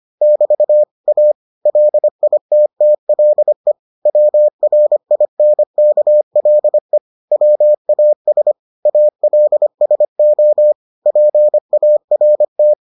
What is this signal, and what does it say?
Morse code, heard as a keyed tone, about 25 wpm, 600 Hz. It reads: <BT> A LITTLE WRINKLE WAS ALSO PART